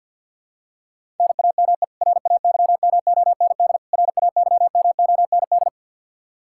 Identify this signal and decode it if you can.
Morse code, heard as a keyed tone, about 49 wpm, 700 Hz. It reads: NAME RAYMOND RAYMOND
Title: Practice QSO - Exchange 1 - Beginning - 50wpm